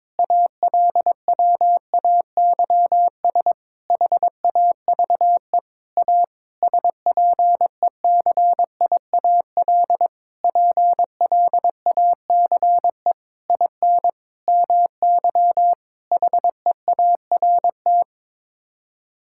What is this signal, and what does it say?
Morse code, heard as a keyed tone, about 22 wpm, 700 Hz. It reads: ALWAYS HAVE A SPECIAL PLACE IN MY HEART
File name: Sets of 9 Words - 22wpm